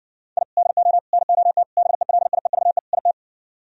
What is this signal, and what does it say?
Morse code, heard as a keyed tone, about 60 wpm, 700 Hz. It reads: I DO NOT BELIEVE IT